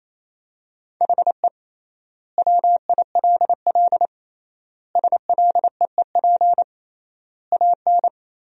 Morse code, heard as a keyed tone, about 28 wpm, 700 Hz. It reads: HE WILL SLEEP AN
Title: 3x Word-Spacing - Sentences from Top 500 Words - 28wpm